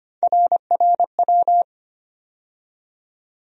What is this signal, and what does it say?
Morse code, heard as a keyed tone, about 25 wpm, 700 Hz. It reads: RRW
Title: USA Call Sign Postfixes - 3 Letters 25wpm